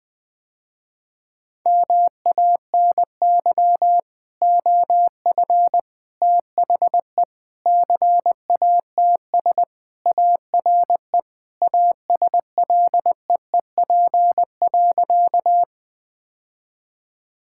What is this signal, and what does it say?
Morse code, heard as a keyed tone, about 21 wpm, 700 Hz. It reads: MANY OF THE CATS ARE ASLEEP.
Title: Sentences with 3-Word Phrases - Spoken Beginning - 20wpm